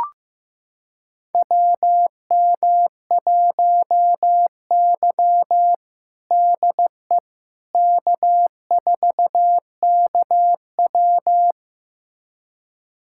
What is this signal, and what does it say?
Morse code, heard as a keyed tone, about 15 wpm, 700 Hz. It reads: WM1Y DE K4KW
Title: USA Callsign DE Callsign - 4 Characters - 15wpm